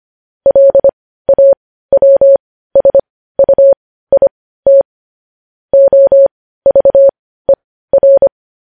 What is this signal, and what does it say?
Morse code, heard as a keyed tone, about 25 wpm, spaced as for 15 wpm, 550 Hz. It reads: LAWSUIT OVER
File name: podcast / News headlines-1x-25x15